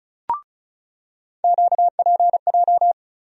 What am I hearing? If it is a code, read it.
Morse code, heard as a keyed tone, about 36 wpm, 700 Hz. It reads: QPJ
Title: USA Call Sign Postfixes - 3 Letters 35wpm